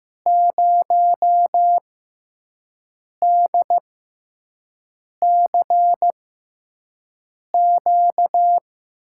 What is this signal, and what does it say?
Morse code, heard as a keyed tone, about 15 wpm, spaced as for 5 wpm, 700 Hz. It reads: 0DCQ
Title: USA Call Signs - 6x Character Spacing - 15wpm